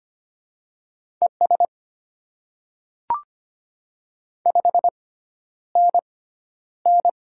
Morse code, heard as a keyed tone, about 25 wpm, spaced as for 11 wpm, 700 Hz. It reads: ES 5NN
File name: Top 20 QSO Elements - 6x Character Spacing - 25wpm